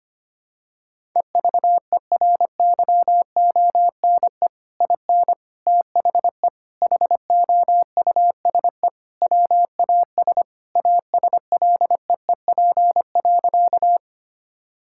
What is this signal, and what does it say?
Morse code, heard as a keyed tone, about 25 wpm, 700 Hz. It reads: EVERYONE IN THE HOUSE WAS ASLEEP.
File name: practice / 25wpm / Sentences with 3-Word Phrases - 25wpm